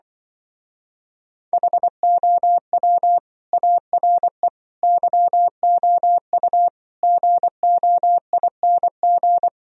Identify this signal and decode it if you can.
Morse code, heard as a keyed tone, about 24 wpm, 700 Hz. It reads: HOW ARE YOU GOING
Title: Speed-Racing - Sets of 5 Words 20wpm